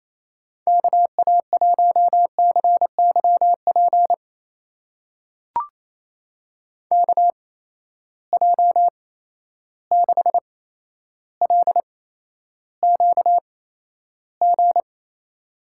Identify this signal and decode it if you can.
Morse code, heard as a keyed tone, about 28 wpm, spaced as for 12 wpm, 700 Hz. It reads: KA1CYP KJ6LQG
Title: USA Call Signs - 6 Characters - 8x Character Spacing - 28wpm